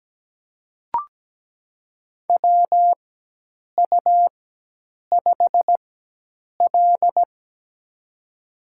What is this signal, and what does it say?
Morse code, heard as a keyed tone, about 17 wpm, spaced as for 8 wpm, 700 Hz. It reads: WU5L
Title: USA Call Signs - 4x Character Spacing - 17wpm